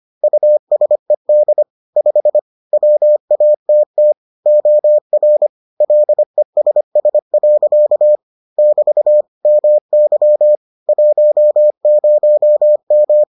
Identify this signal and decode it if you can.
Morse code, heard as a keyed tone, about 25 wpm, 600 Hz. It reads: USED 5 WATT OR LESS. <BT> MY 10M